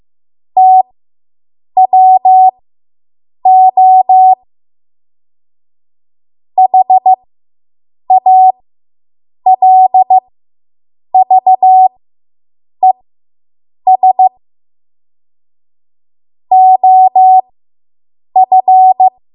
Morse code, heard as a keyed tone, about 15 wpm, spaced as for 7 wpm, 750 Hz. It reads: TWO HALVES OF